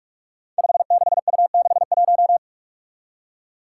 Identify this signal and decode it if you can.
Morse code, heard as a keyed tone, about 46 wpm, 700 Hz. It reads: 56U61
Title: Five Challenging Letter-Number Combinations 45wpm